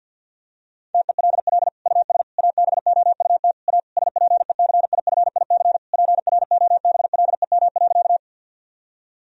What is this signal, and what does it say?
Morse code, heard as a keyed tone, about 52 wpm, 700 Hz. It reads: TELL US ABOUT A SPECIFIC PROBLEM.
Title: Sentences from Top 1000 Words - Review Part 3 - 50wpm